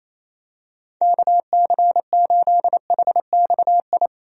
Morse code, heard as a keyed tone, about 28 wpm, 700 Hz. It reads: KC8HXI
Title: USA Call Signs - 6 Characters - 28wpm